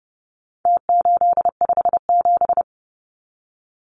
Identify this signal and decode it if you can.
Morse code, heard as a keyed tone, about 30 wpm, 700 Hz. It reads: T857
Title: Numbers - Contest Style - 4-Digit 30wpm